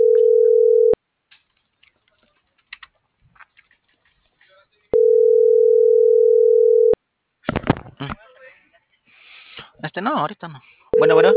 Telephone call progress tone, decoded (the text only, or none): ringback tone